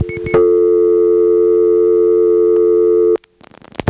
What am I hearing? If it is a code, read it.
Dial tone (North America), 350 + 440 Hz.
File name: GroupBell / MariettaMap-unfinished / Marietta1